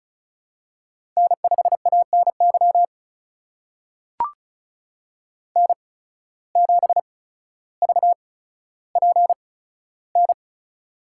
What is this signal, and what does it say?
Morse code, heard as a keyed tone, about 36 wpm, spaced as for 15 wpm, 700 Hz. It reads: N5ANY N7VPN
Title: USA Call Signs - 5 Characters - 8x Character Spacing - 35wpm